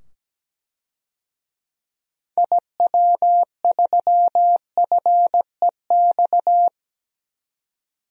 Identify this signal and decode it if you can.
Morse code, heard as a keyed tone, about 17 wpm, 700 Hz. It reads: IW3FEX
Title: DXCC-100 Call Signs - 6 Characters - 17wpm